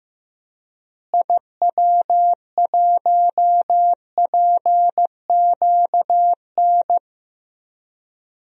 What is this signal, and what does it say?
Morse code, heard as a keyed tone, about 15 wpm, 700 Hz. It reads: IW1PQN
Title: DXCC-100 Call Signs - 6 Characters - Repeated 2x - 15wpm